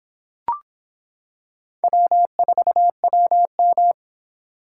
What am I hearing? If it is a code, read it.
Morse code, heard as a keyed tone, about 24 wpm, 700 Hz. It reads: W4WM